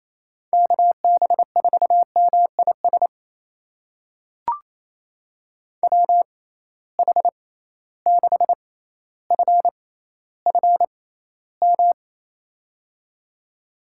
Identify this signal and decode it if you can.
Morse code, heard as a keyed tone, about 28 wpm, spaced as for 14 wpm, 700 Hz. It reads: KB4MIS WH6FFM
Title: USA Call Signs - 6 Characters - 6x Character Spacing - 28wpm